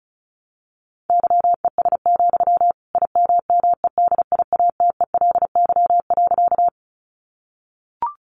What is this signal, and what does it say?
Morse code, heard as a keyed tone, about 34 wpm, 700 Hz. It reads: YES, IMMEDIATELY.
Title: Sentences from Top 1100 Words 35wpm